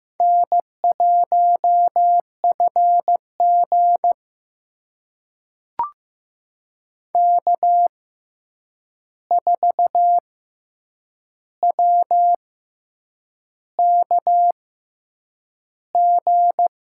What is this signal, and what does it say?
Morse code, heard as a keyed tone, about 15 wpm, spaced as for 7 wpm, 700 Hz. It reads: N1FG K4WKG
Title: USA Call Signs - 6x Character Spacing - 15wpm